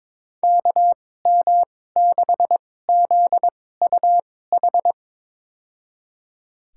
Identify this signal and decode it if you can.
Morse code, heard as a keyed tone, about 21 wpm, spaced as for 16 wpm, 700 Hz. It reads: KM6ZUH
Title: USA Call Signs - 6 Characters - 2x Character Spacing - 22wpm